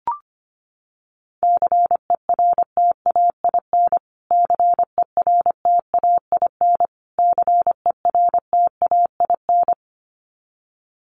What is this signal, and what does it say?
Morse code, heard as a keyed tone, about 25 wpm, 700 Hz. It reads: CERTAIN CERTAIN CERTAIN
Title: Top 500 Words Repeated 25wpm